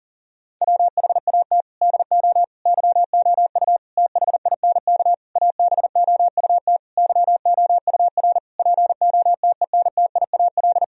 Morse code, heard as a keyed tone, about 40 wpm, 700 Hz. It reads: WHAT DO YOU THINK ABOUT YOUR POTENTIAL
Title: Sentences from Top 1000 Words - Review Part 4 - 40wpm